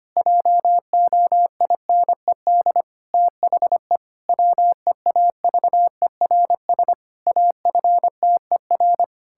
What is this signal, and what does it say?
Morse code, heard as a keyed tone, about 25 wpm, 700 Hz. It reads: JOINED THE WEAVERS AFTER
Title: Pangram Challenge v2 - 25wpm